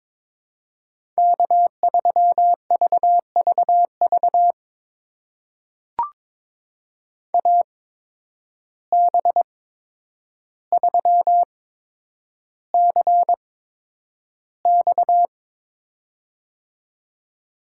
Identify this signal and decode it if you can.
Morse code, heard as a keyed tone, about 22 wpm, spaced as for 9 wpm, 700 Hz. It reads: K3VVV AB3CX
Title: USA Call Signs - 5 Characters - 8x Character Spacing - 22wpm